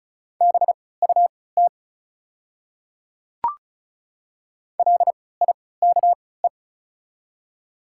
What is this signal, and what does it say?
Morse code, heard as a keyed tone, about 34 wpm, spaced as for 20 wpm, 700 Hz. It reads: BUT LIKE
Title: 3x Character Spacing - Top 100 Words - 35wpm